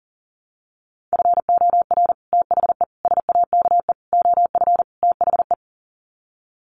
Morse code, heard as a keyed tone, about 40 wpm, 700 Hz. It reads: FOR THE SAKE OF THE
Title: Sets of 5 Words Encore 40wpm